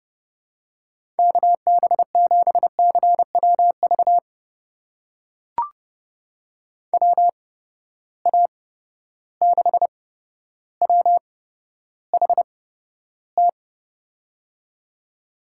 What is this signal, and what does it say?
Morse code, heard as a keyed tone, about 31 wpm, spaced as for 13 wpm, 700 Hz. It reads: KB7CWV WA6WHT
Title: USA Call Signs - 6 Characters - 8x Character Spacing - 30wpm